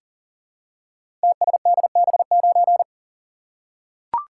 Morse code, heard as a keyed tone, about 38 wpm, 700 Hz. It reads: TSDB9
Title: Code Groups - 5 Letter:Numbers - 40wpm